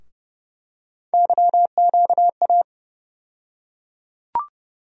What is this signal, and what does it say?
Morse code, heard as a keyed tone, about 30 wpm, 700 Hz. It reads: YQA
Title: USA Call Sign Postfixes - 3 Letters 30wpm